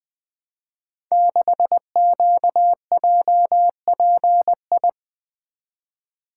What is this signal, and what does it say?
Morse code, heard as a keyed tone, about 20 wpm, 700 Hz. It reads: 6QJPI
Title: Code Groups - 5 Letter:Numbers - 20wpm